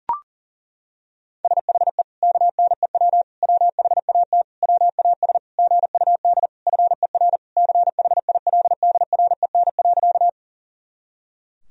Morse code, heard as a keyed tone, about 37 wpm, 700 Hz. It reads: SHE KNEW WHAT WAS GUD FER CHILDREN.
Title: Sentences from Top 400 Words and Abbreviations 40wpm